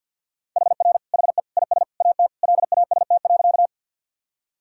Morse code, heard as a keyed tone, about 52 wpm, 700 Hz. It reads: HR HE IS AT LAST.
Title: Sentences from Top 300 Words and Abbreviations 50wpm